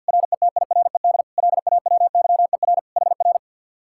Morse code, heard as a keyed tone, about 62 wpm, 700 Hz. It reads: RETIRED LAWYER HR
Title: Speed-Racing Common QSO Phrases Encore - 50wpm